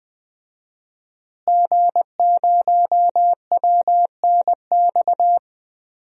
Morse code, heard as a keyed tone, about 21 wpm, 700 Hz. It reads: G0WNX
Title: DXCC-100 Call Signs - 5 Characters - Repeated 3x - 20wpm